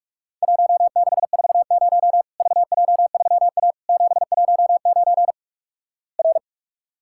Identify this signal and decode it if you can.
Morse code, heard as a keyed tone, about 48 wpm, 700 Hz. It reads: 1640 VJ3A 719 R
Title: Worked All Europe DX Contest - QTC Exchange - 45wpm